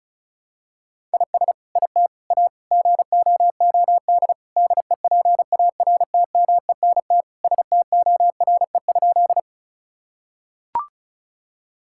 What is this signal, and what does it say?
Morse code, heard as a keyed tone, about 36 wpm, 700 Hz. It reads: IS IT A GOOD DEPARTMENT STORE?
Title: Sentences from Top 1000 Words - Review Part 5 - 35wpm